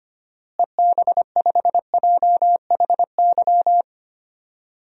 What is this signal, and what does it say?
Morse code, heard as a keyed tone, about 25 wpm, 700 Hz. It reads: EB5JHY
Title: DXCC-100 Call Signs - 6 Characters - Repeated 3x - 25wpm